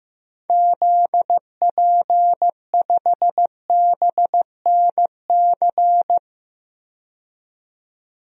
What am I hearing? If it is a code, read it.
Morse code, heard as a keyed tone, about 15 wpm, 700 Hz. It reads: ZP5BNC
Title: DXCC-100 Call Signs - 6 Characters - 15wpm